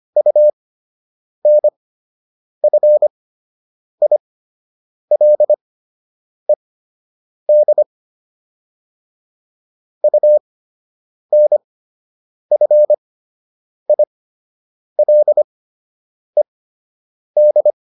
Morse code, heard as a keyed tone, about 25 wpm, spaced as for 8 wpm, 600 Hz. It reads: UNFILED UNFILED